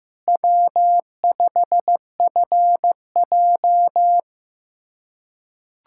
Morse code, heard as a keyed tone, about 15 wpm, 700 Hz. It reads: W5FJ
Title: USA Callsign DE Callsign - 4 Characters - 15wpm